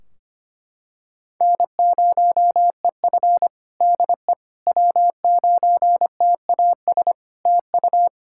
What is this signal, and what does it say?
Morse code, heard as a keyed tone, about 25 wpm, 700 Hz. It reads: N0EF DE W9TAS TU
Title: Practice QSO - Exchange 4 - Ending - 25wpm